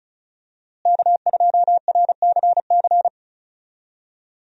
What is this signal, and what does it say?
Morse code, heard as a keyed tone, about 36 wpm, 700 Hz. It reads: K2RCC
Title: USA Call Signs - 5 Characters - Repeated 2x - 35wpm